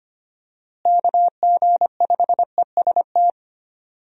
Morse code, heard as a keyed tone, about 25 wpm, 700 Hz. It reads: KG5EST
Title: USA Call Signs - 6 Characters - Repeated 3x - 25wpm